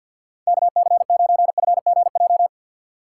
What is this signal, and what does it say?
Morse code, heard as a keyed tone, about 59 wpm, 700 Hz. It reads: KC9FGJ
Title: Speed Racing - USA Call Signs 50wpm